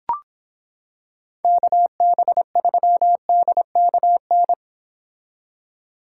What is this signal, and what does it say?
Morse code, heard as a keyed tone, about 26 wpm, 700 Hz. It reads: KB3DKN